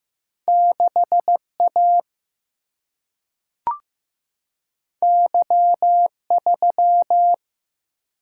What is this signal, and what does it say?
Morse code, heard as a keyed tone, about 15 wpm, 700 Hz. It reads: 6 A Y 3
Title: Single-Letter Single-Number 15wpm